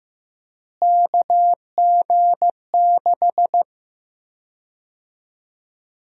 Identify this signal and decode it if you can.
Morse code, heard as a keyed tone, about 15 wpm, 700 Hz. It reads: KG6
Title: USA Call Sign Prefixes - Repeated 2x - 15wpm